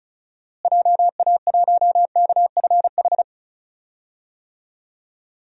Morse code, heard as a keyed tone, about 36 wpm, 700 Hz. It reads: JA1KFH
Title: DXCC-100 Call Signs - 6 Characters - Repeated 3x - 35wpm